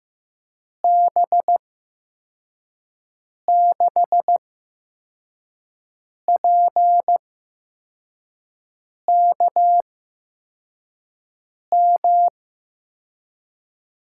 Morse code, heard as a keyed tone, about 15 wpm, spaced as for 4 wpm, 700 Hz. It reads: B6PKM
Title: USA Call Signs - 8x Character Spacing - 15wpm